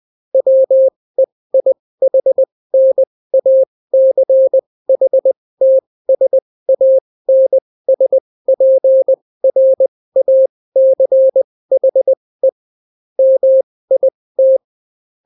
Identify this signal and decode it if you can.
Morse code, heard as a keyed tone, about 20 wpm, spaced as for 16 wpm, 530 Hz. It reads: WEIHNACHTSANSPRACHE MIT